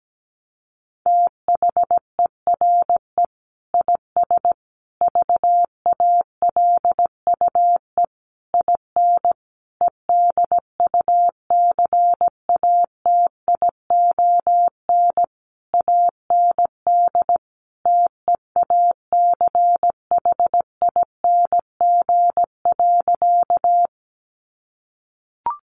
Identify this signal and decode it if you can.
Morse code, heard as a keyed tone, about 17 wpm, 700 Hz. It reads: THERE IS VALUE IN EDUCATION AND TEACHING.
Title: Sentences from Top 1100 Words 17wpm